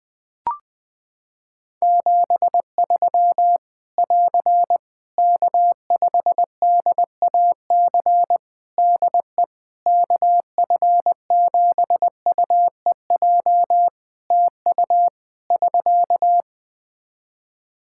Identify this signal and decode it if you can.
Morse code, heard as a keyed tone, about 20 wpm, 700 Hz. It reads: 73 <AR> K5DAC DE KF7UEJ TU <SK>
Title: Practice QSO - Exchange 4 - Ending - 20wpm